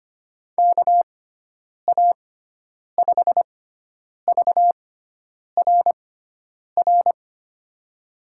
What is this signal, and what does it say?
Morse code, heard as a keyed tone, about 25 wpm, spaced as for 9 wpm, 700 Hz. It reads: KA5VRR
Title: USA Call Signs - 6 Characters - 6x Character Spacing - 25wpm